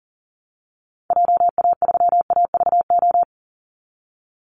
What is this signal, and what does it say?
Morse code, heard as a keyed tone, about 40 wpm, 700 Hz. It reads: JA3AVO
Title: Call Signs - 40wpm